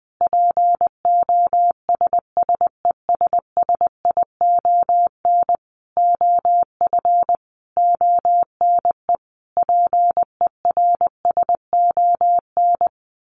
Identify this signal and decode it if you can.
Morse code, heard as a keyed tone, about 20 wpm, 700 Hz. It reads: POSSESSION OF ONE PERSON
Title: Sentences from Top 1400 Words 20wpm